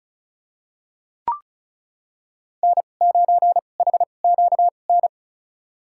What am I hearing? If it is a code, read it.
Morse code, heard as a keyed tone, about 34 wpm, spaced as for 25 wpm, 700 Hz. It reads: N9HQN
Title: USA Call Signs - 2x Character Spacing - 35wpm